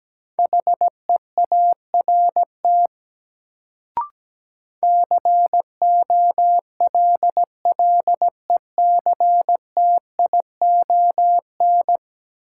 Morse code, heard as a keyed tone, about 17 wpm, 700 Hz. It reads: HEART COLLECTION
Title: Sentences from Top 1200 Words - New Word Drill - 17wpm